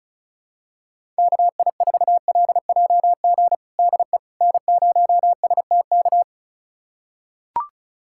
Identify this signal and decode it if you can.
Morse code, heard as a keyed tone, about 36 wpm, 700 Hz. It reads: KI4LJG DE N0STK
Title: USA Callsign DE Callsign 35wpm